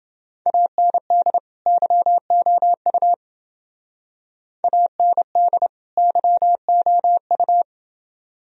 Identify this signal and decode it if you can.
Morse code, heard as a keyed tone, about 28 wpm, 700 Hz. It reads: AND YOU AND YOU